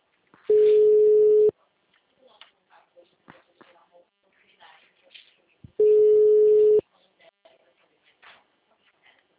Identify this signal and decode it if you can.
Ringback tone (Europe, 425 Hz), with speech.